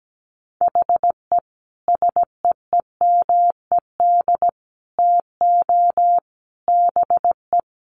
Morse code, heard as a keyed tone, about 17 wpm, 700 Hz. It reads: HE SEEMED TO BE
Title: A Sound Of Thunder 17wpm